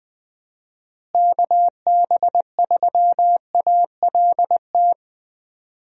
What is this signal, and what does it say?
Morse code, heard as a keyed tone, about 20 wpm, 700 Hz. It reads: KB3ALT
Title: USA Call Signs - 6 Characters - Repeated 3x - 20wpm